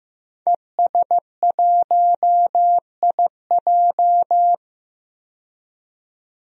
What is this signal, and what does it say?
Morse code, heard as a keyed tone, about 15 wpm, 700 Hz. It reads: ES1IJ
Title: DXCC-100 Call Signs - 5 Characters - 15wpm